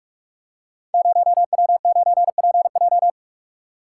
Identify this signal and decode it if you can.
Morse code, heard as a keyed tone, about 44 wpm, 700 Hz. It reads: 0W9PJ
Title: Code Groups - 5 Letter:Numbers - 45wpm